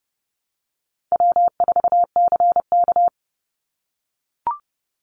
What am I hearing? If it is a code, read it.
Morse code, heard as a keyed tone, about 30 wpm, 700 Hz. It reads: W4CK
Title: Call Signs - 30wpm